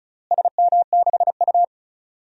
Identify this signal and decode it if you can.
Morse code, heard as a keyed tone, about 36 wpm, 700 Hz. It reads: SM6U
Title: DXCC-100 Call Signs - 4 Characters - Repeated 2x - 35wpm